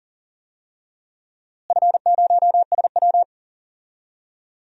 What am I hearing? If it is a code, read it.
Morse code, heard as a keyed tone, about 40 wpm, 700 Hz. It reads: F0SW
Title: DXCC-100 Call Signs - 4 Characters - Repeated 3x - 40wpm